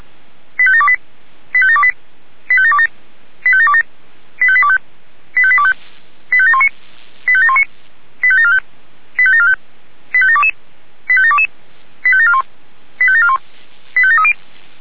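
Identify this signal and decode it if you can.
Five-tone selective call (ZVEI), heard as quick runs of five tones, tones of about 70 ms. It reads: 85728 85728 85727 85727 85825 85825 85719 85719 85745 85745 85722 85722 85741 85741 85730
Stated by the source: ZVEI five-tone selective call received by SDR.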